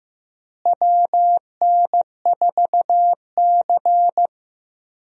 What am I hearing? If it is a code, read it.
Morse code, heard as a keyed tone, about 15 wpm, 700 Hz. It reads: WN4C
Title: USA Call Signs - 4 Characters - Repeated 3x - 15wpm